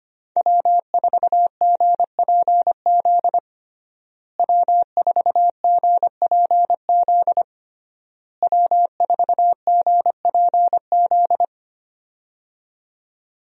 Morse code, heard as a keyed tone, about 25 wpm, 700 Hz. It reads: W4GPZ W4GPZ W4GPZ
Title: USA Call Signs - 5 Characters - Repeated 3x - 25wpm